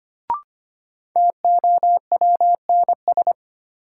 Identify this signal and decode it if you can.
Morse code, heard as a keyed tone, about 25 wpm, 700 Hz. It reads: TOWNS
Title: Sentences from Top 1000 Words - Review Part 6 - New Word Drill - 25wpm